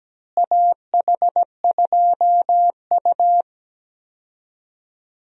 Morse code, heard as a keyed tone, about 17 wpm, 700 Hz. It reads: AH2U
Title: USA Call Signs - 4 Characters - Repeated 3x - 17wpm